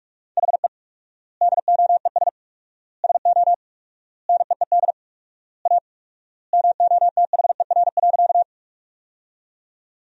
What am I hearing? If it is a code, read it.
Morse code, heard as a keyed tone, about 46 wpm, 700 Hz. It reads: HE DOES SO NEED A MOTHER.
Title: 4x Word-Spacing - Sentences from Top 200 Words - 45wpm